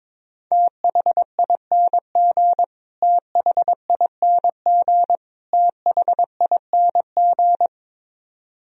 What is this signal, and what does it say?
Morse code, heard as a keyed tone, about 22 wpm, 700 Hz. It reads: THING THING THING
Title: Top 100 Words Repeated 22wpm